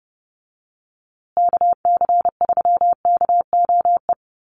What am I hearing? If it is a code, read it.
Morse code, heard as a keyed tone, about 29 wpm, 700 Hz. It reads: KC3KOE
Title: USA Call Signs Repeated 3x 30wpm